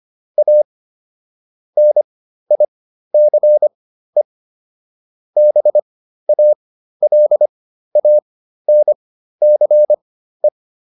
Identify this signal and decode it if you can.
Morse code, heard as a keyed tone, about 25 wpm, spaced as for 13 wpm, 600 Hz. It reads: A NICE BALANCE